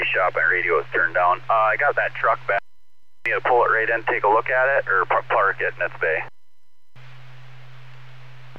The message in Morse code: none

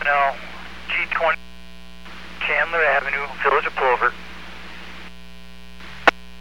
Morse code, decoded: none

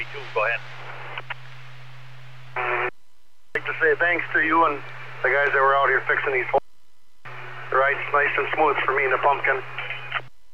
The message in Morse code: none

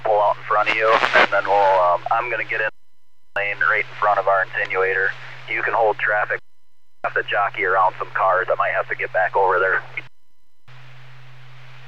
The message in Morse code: none

